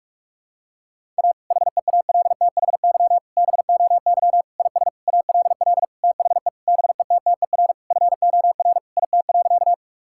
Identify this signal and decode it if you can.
Morse code, heard as a keyed tone, about 41 wpm, 700 Hz. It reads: A HEALTHY BOY IS ALL THE BETTER FOR IT.
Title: Sentences from Top 1000 Words - Review Part 5 - 45wpm